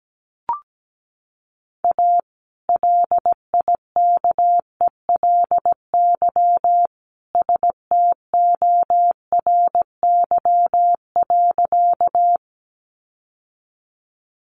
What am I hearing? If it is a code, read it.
Morse code, heard as a keyed tone, about 17 wpm, 700 Hz. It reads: A LIKELY STORY.
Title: Sentences from Top 600 Words 17wpm